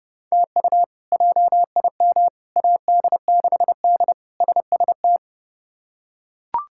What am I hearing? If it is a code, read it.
Morse code, heard as a keyed tone, about 30 wpm, 700 Hz. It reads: TU JIM AD6D SST
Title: K1USN SST QSO Practice - 30wpm